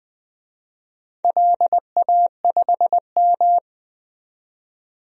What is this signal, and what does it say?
Morse code, heard as a keyed tone, about 20 wpm, 700 Hz. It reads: LA5M
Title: DXCC-100 Call Signs - 4 Characters - 20wpm